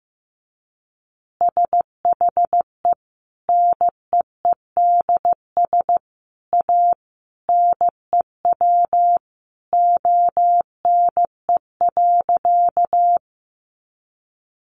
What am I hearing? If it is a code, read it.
Morse code, heard as a keyed tone, about 15 wpm, 700 Hz. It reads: SHE NEEDS A NEW ONE.
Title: Additional Easy Phrases 15wpm